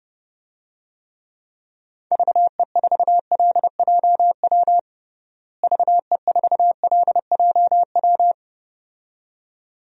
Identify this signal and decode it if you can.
Morse code, heard as a keyed tone, about 30 wpm, 700 Hz. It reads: VE4LJW VE4LJW
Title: DXCC-100 Call Signs - 6 Characters - Repeated 2x - 30wpm